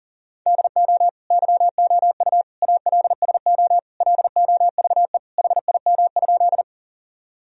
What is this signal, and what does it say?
Morse code, heard as a keyed tone, about 40 wpm, 700 Hz. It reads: DO YOU ALSO LOVE HIM?
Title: Sentences from Top 1000 Words - Review Part 1 - 40wpm